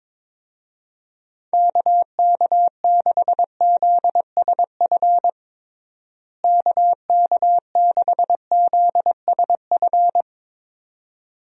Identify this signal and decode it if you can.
Morse code, heard as a keyed tone, about 22 wpm, 700 Hz. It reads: KK6ZSF KK6ZSF
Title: USA Call Signs - 6 Characters - Repeated 2x - 22wpm